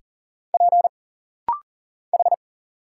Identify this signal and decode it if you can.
Morse code, heard as a keyed tone, about 40 wpm, 700 Hz. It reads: P H
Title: Single Letters - Rapid-Fire 40wpm